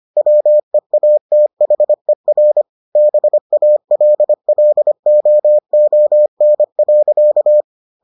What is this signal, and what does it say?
Morse code, heard as a keyed tone, about 25 wpm, 600 Hz. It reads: WEATHER BALLOON.